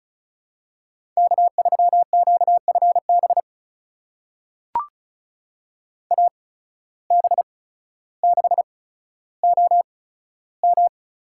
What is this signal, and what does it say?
Morse code, heard as a keyed tone, about 36 wpm, spaced as for 15 wpm, 690 Hz. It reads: K3QFB AB6OM
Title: USA Call Signs - 5 Characters - 8x Character Spacing - 35wpm